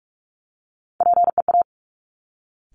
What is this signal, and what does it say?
Morse code, heard as a keyed tone, about 46 wpm, 700 Hz. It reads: PEA
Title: Top Three Letter Combinations in Words 45wpm